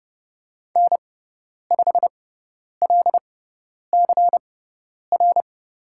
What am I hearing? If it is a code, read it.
Morse code, heard as a keyed tone, about 30 wpm, spaced as for 10 wpm, 700 Hz. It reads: N5LCR
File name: course / 30:10wpm / 102 Lesson 17g - Letter u - Call Signs 30wpm